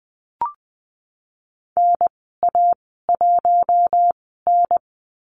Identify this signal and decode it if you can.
Morse code, heard as a keyed tone, about 20 wpm, spaced as for 15 wpm, 700 Hz. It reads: NA1N
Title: 048 Lesson 9g - Number 4 - Call Signs 20wpm